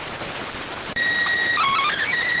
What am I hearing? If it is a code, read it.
SSTV (slow-scan television), PD 290.